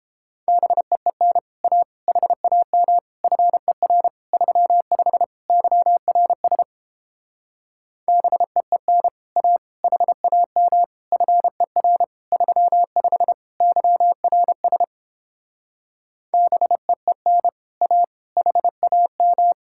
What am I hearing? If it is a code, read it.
Morse code, heard as a keyed tone, about 29 wpm, 700 Hz. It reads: BEEN A HAM FER 35 YRS BEEN A HAM FER 35 YRS BEEN A HAM